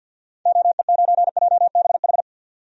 Morse code, heard as a keyed tone, about 52 wpm, 700 Hz. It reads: OE9JBH
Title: DXCC-100 Call Signs - 6 Characters - 50wpm